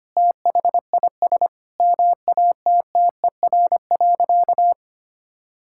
Morse code, heard as a keyed tone, about 25 wpm, 700 Hz. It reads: THIS MATTER.
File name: practice / 25wpm / Sentences with 2-Word Phrases - 25wpm